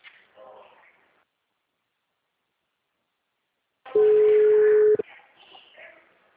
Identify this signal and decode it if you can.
Ringback tone (Europe, 425 Hz), with speech.